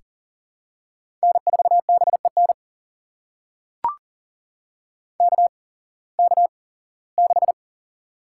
Morse code, heard as a keyed tone, about 39 wpm, spaced as for 21 wpm, 700 Hz. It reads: N4BEN KK6
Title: USA Call Signs - 5 Characters - 8x Character Spacing - 40wpm